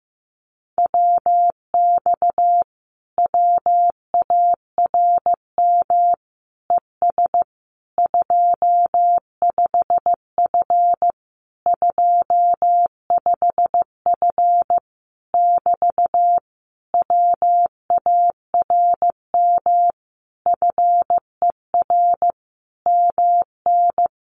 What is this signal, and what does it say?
Morse code, heard as a keyed tone, about 15 wpm, 700 Hz. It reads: WX WARM ES 25F 25F <BT> WARM FER MN
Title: Common QSO Phrases Encore 15wpm